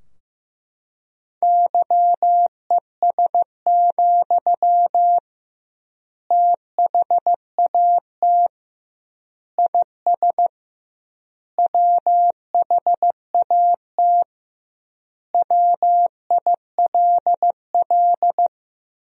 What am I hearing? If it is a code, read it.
Morse code, heard as a keyed tone, about 16 wpm, 700 Hz. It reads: YES, THAT IS WHAT WILL
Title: 2x Word-Spacing - Sentences from Top 500 Words - 15wpm